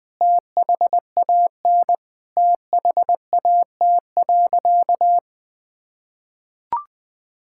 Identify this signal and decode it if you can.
Morse code, heard as a keyed tone, about 20 wpm, 700 Hz. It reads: THAN THAT.
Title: Sentences from Top 1600 Words 20wpm